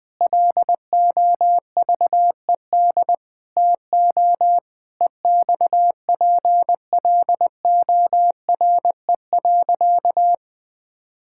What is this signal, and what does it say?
Morse code, heard as a keyed tone, about 20 wpm, 700 Hz. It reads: LOVED TO EXPLORE.
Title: Sentences with 6-Word Phrases - 20wpm